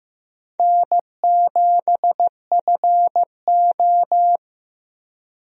Morse code, heard as a keyed tone, about 15 wpm, 700 Hz. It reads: N7FO
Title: USA Call Signs - 4 Characters - Repeated 2x - 15wpm